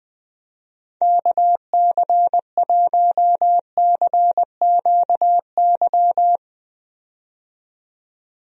Morse code, heard as a keyed tone, about 20 wpm, 700 Hz. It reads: KC1CQY
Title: USA Call Signs - 6 Characters - 20wpm